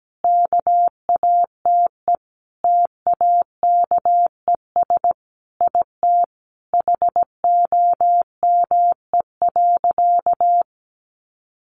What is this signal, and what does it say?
Morse code, heard as a keyed tone, about 17 wpm, 700 Hz. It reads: KATE TAKES IT HOME.
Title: Easy Phrases 17wpm